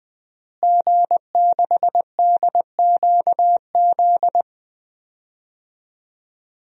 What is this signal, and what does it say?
Morse code, heard as a keyed tone, about 21 wpm, 700 Hz. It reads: G6DQZ
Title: DXCC-100 Call Signs - 5 Characters - Repeated 2x - 20wpm